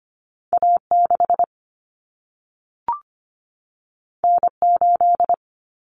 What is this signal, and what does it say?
Morse code, heard as a keyed tone, about 25 wpm, 700 Hz. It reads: A6 N8
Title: Common Cut Numbers - 2-Digit 25wpm